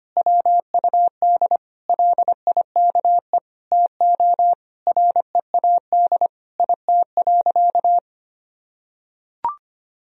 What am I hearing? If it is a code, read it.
Morse code, heard as a keyed tone, about 25 wpm, 700 Hz. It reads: WUD LIKE TO READ IT.
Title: Sentences from Top 500 Words and Abbreviations 25wpm